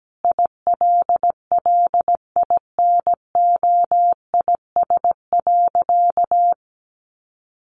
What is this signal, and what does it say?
Morse code, heard as a keyed tone, about 17 wpm, 700 Hz. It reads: ILLINOIS.
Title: A Sound Of Thunder 17wpm